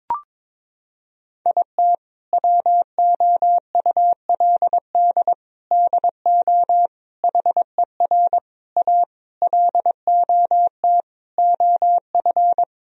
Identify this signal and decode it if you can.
Morse code, heard as a keyed tone, about 22 wpm, 700 Hz. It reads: IT WOULD DO HER A LOT OF
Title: Sentences from Top 1000 Words - Review Part 3 - 22wpm